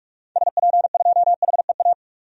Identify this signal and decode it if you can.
Morse code, heard as a keyed tone, about 46 wpm, 700 Hz. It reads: SP2HEA
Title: DXCC-100 Call Signs - 6 Characters - 45wpm